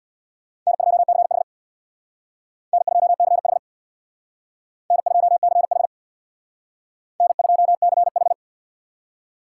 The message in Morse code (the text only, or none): N2CH N2CH N2CH N2CH